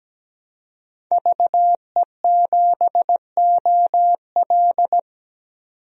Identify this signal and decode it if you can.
Morse code, heard as a keyed tone, about 17 wpm, 700 Hz. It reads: VE7OL
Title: DXCC-100 Call Signs - 5 Characters - 17wpm